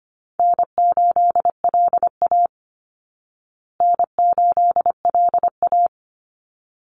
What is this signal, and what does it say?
Morse code, heard as a keyed tone, about 25 wpm, 700 Hz. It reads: N8LA N8LA
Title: Easy Call Signs Repeated 2x - 25wpm